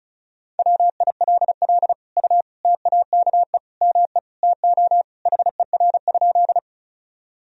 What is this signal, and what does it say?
Morse code, heard as a keyed tone, about 36 wpm, 700 Hz. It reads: WILL U TAKE ME TO HER?
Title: Sentences from Top 200 Words and Abbreviations 35wpm